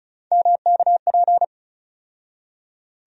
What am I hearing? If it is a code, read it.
Morse code, heard as a keyed tone, about 36 wpm, 700 Hz. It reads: MKP
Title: USA Call Sign Postfixes - 3 Letters 35wpm